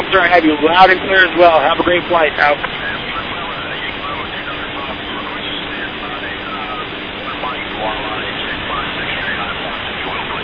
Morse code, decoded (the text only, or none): none